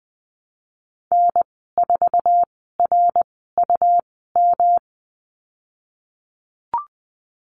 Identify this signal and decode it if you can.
Morse code, heard as a keyed tone, about 20 wpm, spaced as for 15 wpm, 700 Hz. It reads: N4RUM
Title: 140 Lesson 23g - Letter f - Call Signs 20wpm